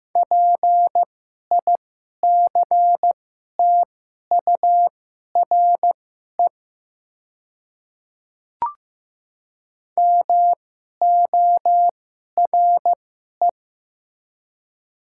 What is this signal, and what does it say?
Morse code, heard as a keyed tone, about 15 wpm, spaced as for 11 wpm, 700 Hz. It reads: PICTURE MORE
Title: 2x Character Spacing - Top 200 Words - 15wpm